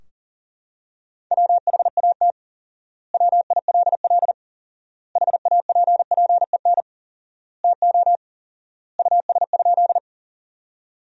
Morse code, heard as a keyed tone, about 40 wpm, 700 Hz. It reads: WHAT WILL HAPPEN TO US?
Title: 4x Word-Spacing - Sentences from Top 400 Words - 40wpm